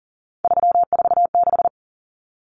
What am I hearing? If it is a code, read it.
Morse code, heard as a keyed tone, about 40 wpm, 700 Hz. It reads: 346
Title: Common Cut Numbers - 3-Digit 40wpm